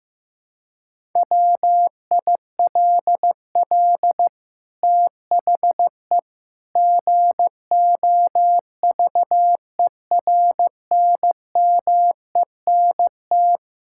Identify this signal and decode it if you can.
Morse code, heard as a keyed tone, about 15 wpm, 700 Hz. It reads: WILL THE GOVERNMENT
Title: Sentences from Top 1000 Words - Review Part 5 - 15wpm